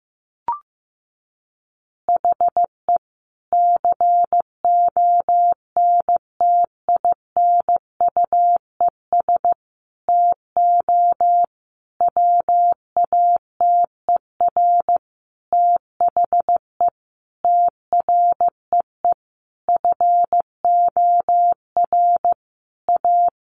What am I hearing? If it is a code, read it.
Morse code, heard as a keyed tone, about 15 wpm, 700 Hz. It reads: HE CONTINUES TO WATER THE TREE FOR A
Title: Easy Phrases 15wpm